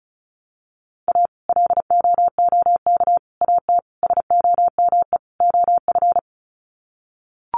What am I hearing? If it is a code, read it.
Morse code, heard as a keyed tone, about 34 wpm, 700 Hz. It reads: A LOOK AT SOME OF
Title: Sets of 5 Words Encore2 35wpm